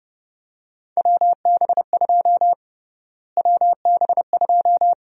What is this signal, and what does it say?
Morse code, heard as a keyed tone, about 30 wpm, 700 Hz. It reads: WB2 WB2
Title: USA Call Sign Prefixes - Repeated 2x - 30wpm